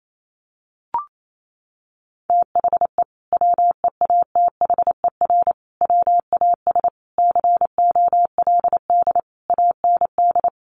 Morse code, heard as a keyed tone, about 28 wpm, 700 Hz. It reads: THE WEATHER WAS COLD AND
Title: Sentences from Top 1500 Words 28wpm